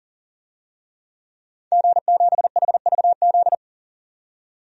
Morse code, heard as a keyed tone, about 40 wpm, 700 Hz. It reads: G7HVZ
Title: DXCC-100 Call Signs - 5 Characters - 40wpm